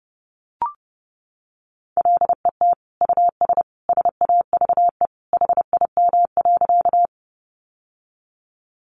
Morse code, heard as a keyed tone, about 30 wpm, 700 Hz. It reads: LET US SAVE HIM.
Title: Sentences from Top 1100 Words 30wpm